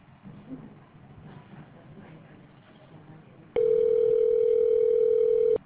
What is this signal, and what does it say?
Ringback tone (North America), 440 + 480 Hz.